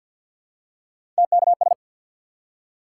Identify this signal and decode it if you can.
Morse code, heard as a keyed tone, about 48 wpm, 700 Hz. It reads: TKS
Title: Top 90 QSO Elements 50wpm